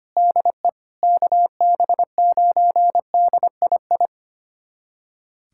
Morse code, heard as a keyed tone, about 25 wpm, 700 Hz. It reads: DE KB9DII
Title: USA Callsign DE Callsign - 6 Characters - 25wpm